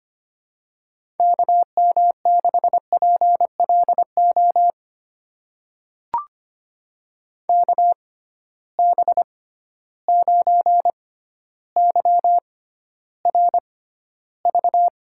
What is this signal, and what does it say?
Morse code, heard as a keyed tone, about 25 wpm, spaced as for 13 wpm, 700 Hz. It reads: KM6PLO KB9YRV
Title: USA Call Signs - 6 Characters - 6x Character Spacing - 25wpm